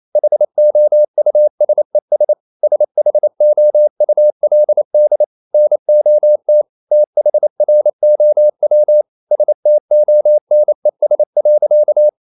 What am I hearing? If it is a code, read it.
Morse code, heard as a keyed tone, about 28 wpm, 600 Hz. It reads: HOUSES SHOULD NOT THROW STONES.
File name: CW / Practice / EnglishSayings28